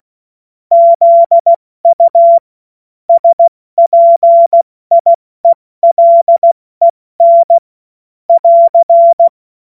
Morse code, heard as a keyed tone, about 16 wpm, 680 Hz. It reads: ZU SPIELEN <AR>
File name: Morsekurs / Witze / Witz 16. Mai 2023 70 F80